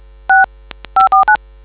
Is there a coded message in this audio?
Yes: DTMF from 0.3 to 1.4 s, heard as telephone touch tones, tones of about 130 ms. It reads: 6 549